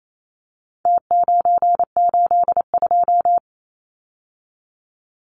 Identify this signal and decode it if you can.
Morse code, heard as a keyed tone, about 28 wpm, 700 Hz. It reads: T982
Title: Numbers - Contest Style - 4-Digit 28wpm